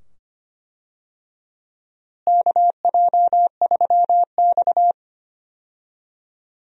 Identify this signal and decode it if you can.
Morse code, heard as a keyed tone, about 25 wpm, 700 Hz. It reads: KJ3X
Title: USA Call Signs - 4 Characters - Repeated 2x - 25wpm